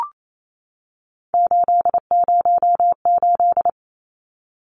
Morse code, heard as a keyed tone, about 28 wpm, 700 Hz. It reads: 808
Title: Numbers - 3-Digit 28wpm